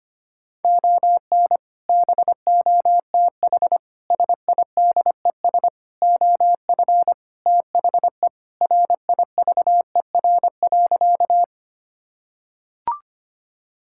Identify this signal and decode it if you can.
Morse code, heard as a keyed tone, about 25 wpm, 700 Hz. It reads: ON BOTH SIDES OF THE RIVER.
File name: practice / 25wpm / Sentences with 4-Word Phrases - 25wpm